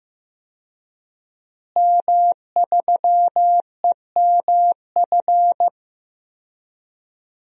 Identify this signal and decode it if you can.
Morse code, heard as a keyed tone, about 15 wpm, 700 Hz. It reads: M3EMF
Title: DXCC-100 Call Signs - 5 Characters - Repeated 3x - 15wpm